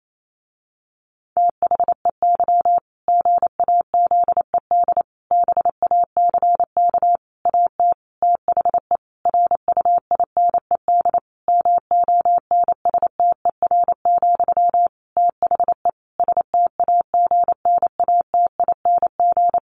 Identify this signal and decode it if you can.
Morse code, heard as a keyed tone, about 28 wpm, 700 Hz. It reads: THEY GAZED BACK AT THE RUINED MONSTER, THE STAGNATING
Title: A Sound Of Thunder 28wpm